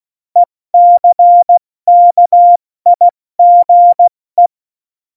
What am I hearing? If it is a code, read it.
Morse code, heard as a keyed tone, about 16 wpm, spaced as for 14 wpm, 700 Hz. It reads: ECKIGE
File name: Morsekurs / Witze / Witz 14 April 2020, 70 F80